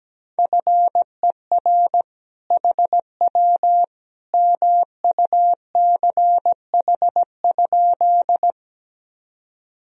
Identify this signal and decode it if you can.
Morse code, heard as a keyed tone, about 17 wpm, 700 Hz. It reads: FER HW MUCH?
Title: Sentences from Top 200 Words and Abbreviations 17wpm